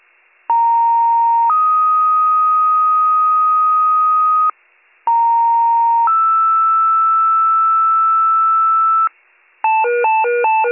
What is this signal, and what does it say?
Two-tone page, heard as a one-second tone, then a three-second tone, about 930 then 1250 Hz, about 930 then 1320 Hz. It is heard from 0.5 to 9.1 s.